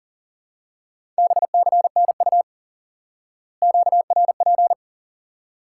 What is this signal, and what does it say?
Morse code, heard as a keyed tone, about 40 wpm, 700 Hz. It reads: BCNU QRP
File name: IQE Course 3 - Lesson 89 - YAGI - Two Elements - 40wpm